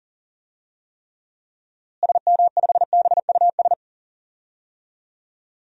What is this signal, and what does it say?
Morse code, heard as a keyed tone, about 40 wpm, 700 Hz. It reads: SM5BUS
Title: DXCC-100 Call Signs - 6 Characters - Repeated 2x - 40wpm